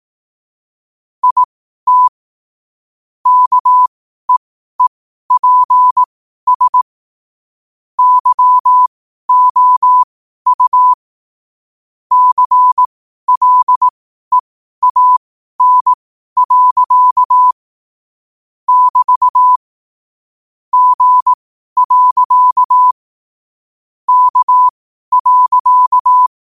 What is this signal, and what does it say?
Morse code, heard as a keyed tone, about 18 wpm, spaced as for 12 wpm, 1000 Hz. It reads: IT KEEPS YOU CLEAN. <BT> G. K.
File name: qotdfiles / 12wpm